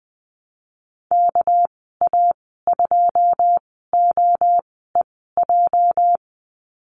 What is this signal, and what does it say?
Morse code, heard as a keyed tone, about 20 wpm, spaced as for 15 wpm, 700 Hz. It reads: KA2OEJ
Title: 207 Lesson 33g - Letter j - Call Signs 20wpm